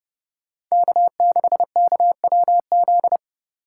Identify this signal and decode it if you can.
Morse code, heard as a keyed tone, about 37 wpm, 700 Hz. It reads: K6KWZ
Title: Speed Racing - USA 1x3 Call Signs 30wpm